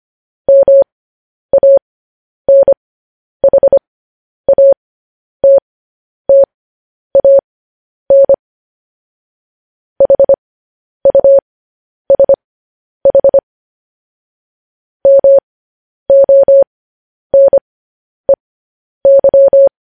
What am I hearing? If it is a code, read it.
Morse code, heard as a keyed tone, about 25 wpm, spaced as for 10 wpm, 550 Hz. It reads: MANHATTAN HUSH MONEY
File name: podcast / News headlines-1x-25x10